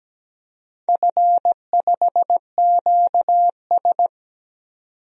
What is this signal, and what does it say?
Morse code, heard as a keyed tone, about 17 wpm, 700 Hz. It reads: F5QS
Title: DXCC-100 Call Signs - 4 Characters - 17wpm